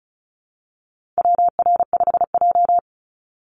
Morse code, heard as a keyed tone, about 34 wpm, 700 Hz. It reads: WR5J
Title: Call Signs - 35wpm